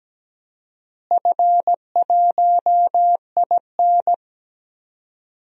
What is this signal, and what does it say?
Morse code, heard as a keyed tone, about 17 wpm, 700 Hz. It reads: F1IN
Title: DXCC-100 Call Signs - 4 Characters - Repeated 2x - 17wpm